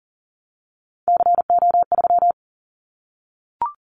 Morse code, heard as a keyed tone, about 40 wpm, 700 Hz. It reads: CO3
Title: Callsign Prefixes for Top 100 DXCC - 40wpm